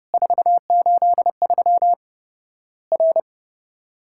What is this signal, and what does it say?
Morse code, heard as a keyed tone, about 30 wpm, 700 Hz. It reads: 483 R
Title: Worked All Europe DX Contest - QTC Exchange - 30wpm